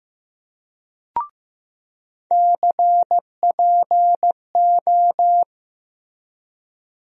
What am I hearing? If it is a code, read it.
Morse code, heard as a keyed tone, about 15 wpm, 700 Hz. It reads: CPO
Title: USA Call Sign Postfixes - 3 Letters 15wpm